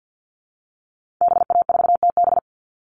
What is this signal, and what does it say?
Morse code, heard as a keyed tone, about 49 wpm, 700 Hz. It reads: BA4TB
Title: Call Signs Repeated 3x - 50wpm